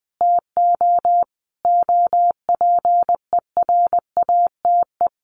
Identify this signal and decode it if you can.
Morse code, heard as a keyed tone, about 20 wpm, 700 Hz. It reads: TO OPERATE
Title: Sentences from Top 1500 Words 20wpm